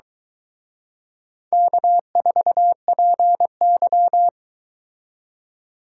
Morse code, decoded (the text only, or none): K4PY